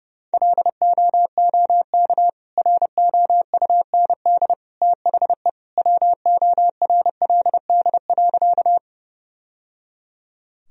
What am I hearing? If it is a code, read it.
Morse code, heard as a keyed tone, about 32 wpm, 700 Hz. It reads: LOOK ROUND THE WORLD.
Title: Sentences from Top 1000 Words - Review Part 1 - 30wpm